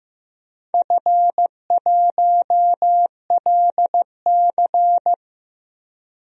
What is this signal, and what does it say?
Morse code, heard as a keyed tone, about 15 wpm, 700 Hz. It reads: F1LC
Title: DXCC-100 Call Signs - 4 Characters - Repeated 3x - 15wpm